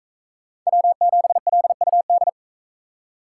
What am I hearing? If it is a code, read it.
Morse code, heard as a keyed tone, about 44 wpm, 700 Hz. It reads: W7LUD